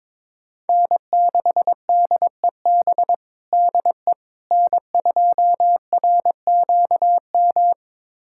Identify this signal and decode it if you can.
Morse code, heard as a keyed tone, about 22 wpm, 700 Hz. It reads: N6DEB DE N2RQM
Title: USA Callsign DE Callsign - 5 Characters - 22wpm